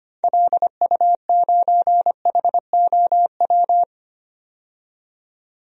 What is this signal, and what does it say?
Morse code, heard as a keyed tone, about 25 wpm, 700 Hz. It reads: LU9HOW
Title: DXCC-100 Call Signs - 6 Characters - 25wpm